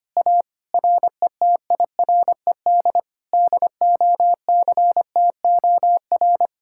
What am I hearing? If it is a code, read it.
Morse code, heard as a keyed tone, about 24 wpm, 700 Hz. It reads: A RETIRED DOCTOR
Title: Speed-Racing Common QSO Phrases Encore - 25wpm